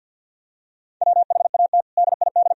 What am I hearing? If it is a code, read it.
Morse code, heard as a keyed tone, about 51 wpm, 700 Hz. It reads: WHAT DID
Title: Sentences from Top 1000 Words - Review Part 4 - 50wpm